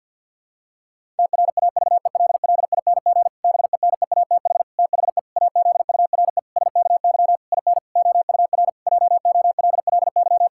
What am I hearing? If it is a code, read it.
Morse code, heard as a keyed tone, about 50 wpm, 700 Hz. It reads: TRAVELLING BENEATH THE AZURE SKY IN OUR JOLLY
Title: Pangram Challenge v2 - 50wpm